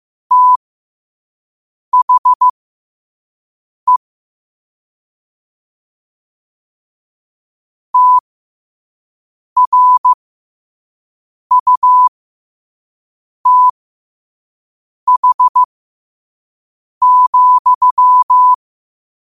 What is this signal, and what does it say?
Morse code, heard as a keyed tone, about 15 wpm, spaced as for 5 wpm, 1000 Hz. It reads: THE TRUTH,
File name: qotdfiles / 5wpm